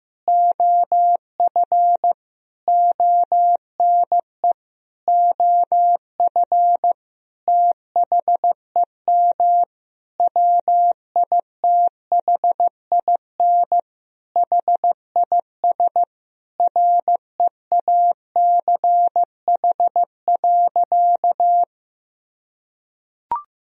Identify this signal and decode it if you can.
Morse code, heard as a keyed tone, about 15 wpm, 700 Hz. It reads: OF ONE OF THEM WITHIN HIS REACH.
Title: Sentences from Top 1000 Words - Review Part 4 - 15wpm